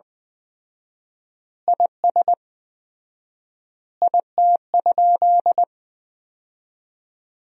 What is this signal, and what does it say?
Morse code, heard as a keyed tone, about 20 wpm, 700 Hz. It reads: IS IT?
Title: 4x Word-Spacing - Sentences from Top 300 Words - 20wpm